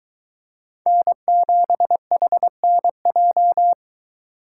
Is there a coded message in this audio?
Yes: Morse code from 0.9 to 3.7 s, heard as a keyed tone, about 23 wpm, 700 Hz. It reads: N7HNJ